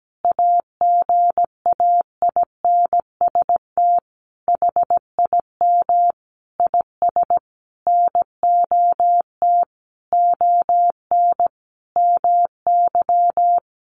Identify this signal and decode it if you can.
Morse code, heard as a keyed tone, about 17 wpm, 700 Hz. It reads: AGAINST HIM IS NOT ON MY
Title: Sentences from Top 1400 Words 17wpm